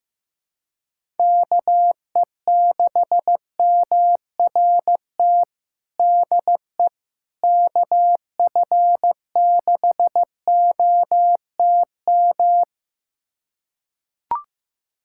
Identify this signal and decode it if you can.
Morse code, heard as a keyed tone, about 15 wpm, 700 Hz. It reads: KE6MRT DE KF6OTM
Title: USA Callsign DE Callsign - 6 Characters - 15wpm